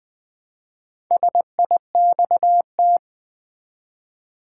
Morse code, heard as a keyed tone, about 20 wpm, 700 Hz. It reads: SIXT
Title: Build A Word - 1 Letter At A Time - 5-Letter Words - 20wpm